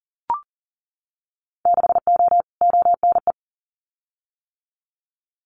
Morse code, heard as a keyed tone, about 40 wpm, 700 Hz. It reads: 6O ONE
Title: Winter Field Day Exchanges Encore 40wpm